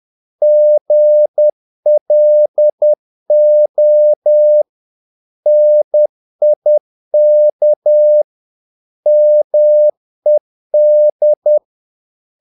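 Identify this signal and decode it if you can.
Morse code, heard as a keyed tone, about 10 wpm, 600 Hz. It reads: GLO NIK MED
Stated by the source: Morse practice of short Danish words at 10 wpm, from CW med Gnister.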